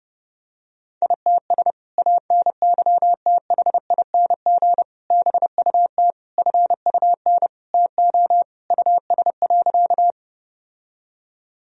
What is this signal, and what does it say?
Morse code, heard as a keyed tone, about 29 wpm, 700 Hz. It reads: ITS ANYTHING BUT FUN TO US.
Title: Sentences from Top 1000 Words - Review Part 4 - 30wpm